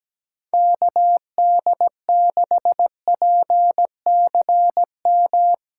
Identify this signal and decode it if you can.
Morse code, heard as a keyed tone, about 17 wpm, 700 Hz. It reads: KD6PCM
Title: USA Callsign DE Callsign 17wpm